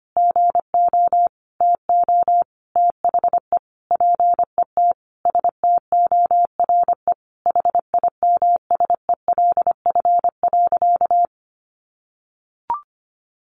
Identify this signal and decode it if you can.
Morse code, heard as a keyed tone, about 25 wpm, 700 Hz. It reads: GO TO THE PET STORE HIMSELF.
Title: Additional Easy Phrases 25wpm